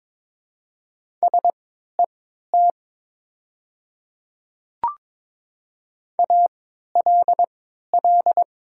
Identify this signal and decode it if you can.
Morse code, heard as a keyed tone, about 22 wpm, spaced as for 13 wpm, 700 Hz. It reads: SET ALL
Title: 3x Character Spacing - Top 200 Words - 22wpm